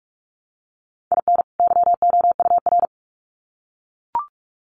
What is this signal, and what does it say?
Morse code, heard as a keyed tone, about 44 wpm, 700 Hz. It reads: IN YOUR
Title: Sets of 2 Words 45wpm